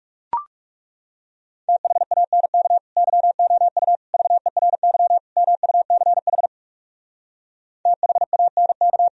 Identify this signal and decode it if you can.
Morse code, heard as a keyed tone, about 40 wpm, 700 Hz. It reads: THANK YOU VERY MUCH THANK